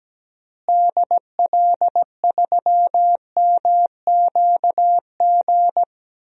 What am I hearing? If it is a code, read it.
Morse code, heard as a keyed tone, about 17 wpm, 700 Hz. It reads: DL3MQG
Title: DXCC-100 Call Signs - 6 Characters - Repeated 3x - 17wpm